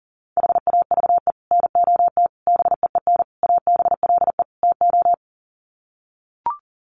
Morse code, heard as a keyed tone, about 40 wpm, 700 Hz. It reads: HAVE NOT BEEN ABLE TO
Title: Sets of 5 Words 40wpm